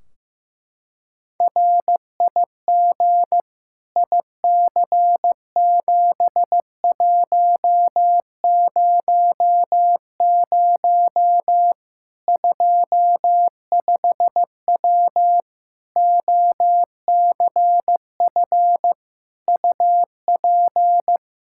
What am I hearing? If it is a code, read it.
Morse code, heard as a keyed tone, about 15 wpm, 700 Hz. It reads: RIG IC7100 25W OCF UP
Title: K1USN SST QSO Practice 15wpm